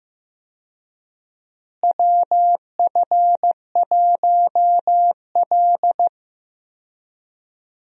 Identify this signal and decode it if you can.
Morse code, heard as a keyed tone, about 15 wpm, 700 Hz. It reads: WF1L
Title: USA Call Signs - 4 Characters - Repeated 3x - 15wpm